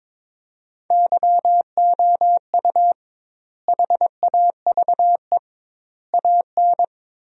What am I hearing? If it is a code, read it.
Morse code, heard as a keyed tone, about 22 wpm, 700 Hz. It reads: YOU HAVE AN
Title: 2x Word-Spacing - Sentences from Top 400 Words - 22wpm